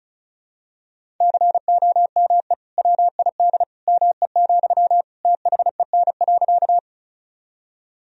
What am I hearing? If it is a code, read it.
Morse code, heard as a keyed tone, about 36 wpm, 700 Hz. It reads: COME WID ME, THEN.
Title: Sentences from Top 200 Words and Abbreviations 35wpm